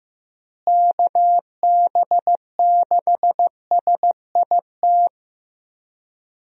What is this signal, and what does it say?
Morse code, heard as a keyed tone, about 15 wpm, 700 Hz. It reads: KB6SIT
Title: USA Call Signs - 6 Characters - Repeated 3x - 15wpm